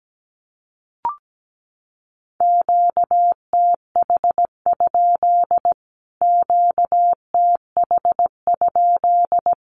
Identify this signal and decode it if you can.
Morse code, heard as a keyed tone, about 17 wpm, 700 Hz. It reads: QTH? QTH?
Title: Q-Codes Repeated 3x 17wpm